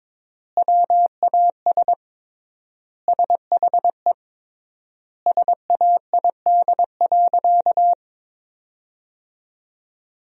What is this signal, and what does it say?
Morse code, heard as a keyed tone, about 21 wpm, 700 Hz. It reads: WAS SHE SAID.
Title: 3x Word-Spacing - Sentences from Top 400 Words - 22wpm